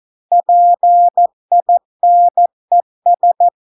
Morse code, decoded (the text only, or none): PINES